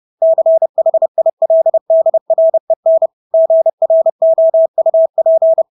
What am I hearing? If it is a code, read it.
Morse code, heard as a keyed tone, about 30 wpm, 650 Hz. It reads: CHILDREN GROUP
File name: Engelske ord 30-30 wpm